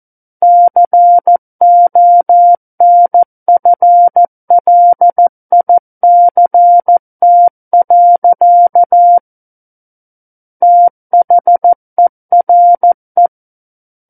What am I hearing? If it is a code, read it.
Morse code, heard as a keyed tone, about 14 wpm, 700 Hz. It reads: CONFLICT. THERE
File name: Book One / 05-Chapter 5 - 14 WPM